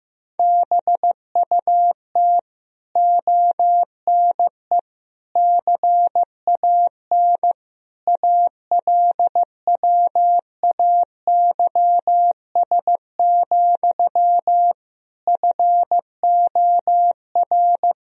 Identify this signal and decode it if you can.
Morse code, heard as a keyed tone, about 16 wpm, 700 Hz. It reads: BUT ONE CAN ALWAYS, FOR
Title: Sentences from Top 1000 Words - Review Part 4 - 15wpm